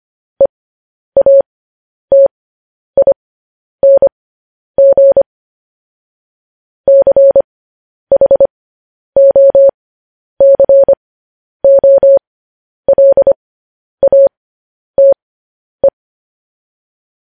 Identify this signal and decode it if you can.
Morse code, heard as a keyed tone, about 25 wpm, spaced as for 10 wpm, 550 Hz. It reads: EATING CHOCOLATE
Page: news headlines in morse code at 10 wpm